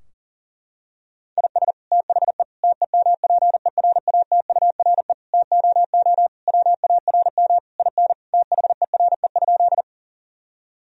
Morse code, heard as a keyed tone, about 40 wpm, 700 Hz. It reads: IS THE TEMPERATURE TOO WARM IN THERE?
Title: Sentences from Top 1000 Words - Review Part 4 - 40wpm